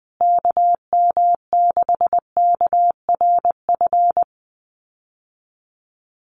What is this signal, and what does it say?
Morse code, heard as a keyed tone, about 20 wpm, 700 Hz. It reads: KM6KRF
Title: USA Call Signs Repeated 3x 20wpm